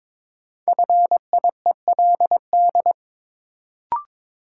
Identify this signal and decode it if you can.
Morse code, heard as a keyed tone, about 22 wpm, 700 Hz. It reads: FIELD
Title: Sentences from Top 1000 Words - New Word Drill - 22wpm